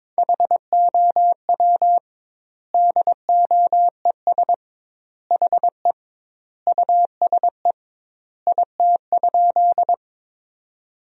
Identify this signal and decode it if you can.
Morse code, heard as a keyed tone, about 22 wpm, 700 Hz. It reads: HOW DOES HE USE IT?
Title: 2x Word-Spacing - Sentences from Top 200 Words - 22wpm